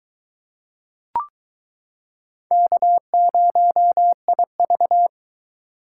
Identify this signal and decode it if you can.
Morse code, heard as a keyed tone, about 22 wpm, 700 Hz. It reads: K0IV